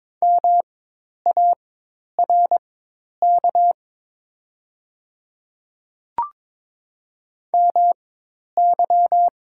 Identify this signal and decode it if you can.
Morse code, heard as a keyed tone, about 22 wpm, spaced as for 10 wpm, 700 Hz. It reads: MARK MY
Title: 4x Character Spacing - Top 500 Words - 22wpm